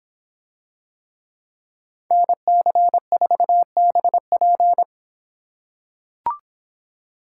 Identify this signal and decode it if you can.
Morse code, heard as a keyed tone, about 25 wpm, 700 Hz. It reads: NC4BP